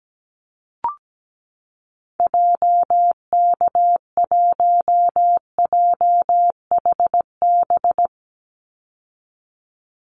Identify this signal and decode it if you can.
Morse code, heard as a keyed tone, about 17 wpm, 700 Hz. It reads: JK1JHB
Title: Call Signs Repeated 2x - 17wpm